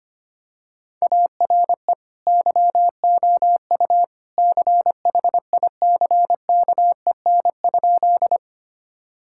Morse code, heard as a keyed tone, about 25 wpm, 700 Hz. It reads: ARE YOU CHICKEN?
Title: Sentences from Top 1000 Words - Review Part 5 - 25wpm